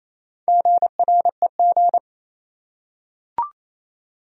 Morse code, heard as a keyed tone, about 28 wpm, 700 Hz. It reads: GREG
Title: USA Common Names - Max 5 Letters - 28wpm